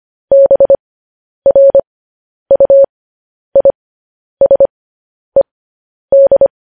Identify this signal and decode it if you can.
Morse code, heard as a keyed tone, about 25 wpm, spaced as for 10 wpm, 550 Hz. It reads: BRUISED
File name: podcast / News headlines-1x-25x10